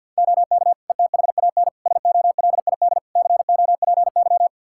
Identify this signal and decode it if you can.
Morse code, heard as a keyed tone, about 46 wpm, 700 Hz. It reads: OK ETHAN SOLID COPY
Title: Practice QSO - Exchange 3 - More Info - 50wpm